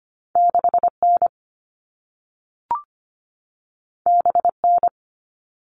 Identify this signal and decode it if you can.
Morse code, heard as a keyed tone, about 25 wpm, 700 Hz. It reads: 6 N B N
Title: All Cut Numbers - 2-Digit 25wpm